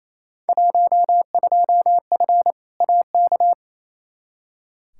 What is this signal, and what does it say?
Morse code, heard as a keyed tone, about 28 wpm, 700 Hz. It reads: 12F AK
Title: Field Day Exchanges Encore - 28wpm